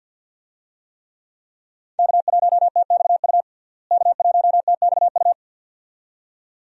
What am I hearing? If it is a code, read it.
Morse code, heard as a keyed tone, about 52 wpm, 700 Hz. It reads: K1TXU K1TXU
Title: USA Call Signs - 5 Characters - Repeated 2x - 50wpm